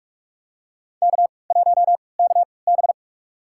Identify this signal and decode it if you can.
Morse code, heard as a keyed tone, about 45 wpm, spaced as for 25 wpm, 700 Hz. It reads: K1KB
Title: USA Call Signs - 3x Character Spacing - 45wpm